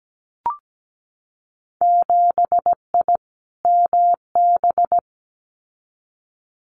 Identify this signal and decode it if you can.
Morse code, heard as a keyed tone, about 17 wpm, 700 Hz. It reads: 7I MB
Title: Winter Field Day Exchanges Encore 17wpm